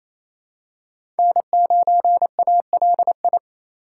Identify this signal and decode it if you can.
Morse code, heard as a keyed tone, about 27 wpm, 700 Hz. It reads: N9ALI
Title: USA Call Signs - 5 Characters - 2x Character Spacing - 28wpm